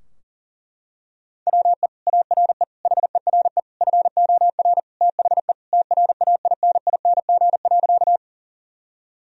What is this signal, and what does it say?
Morse code, heard as a keyed tone, about 37 wpm, 700 Hz. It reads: WE ARE HERE FOR THE TRAINING.
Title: Sentences from Top 1000 Words - Review Part 4 - 40wpm